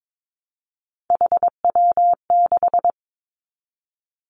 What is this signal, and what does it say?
Morse code, heard as a keyed tone, about 22 wpm, 700 Hz. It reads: HW6
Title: Three Challenging Letter-Number Combinations 22wpm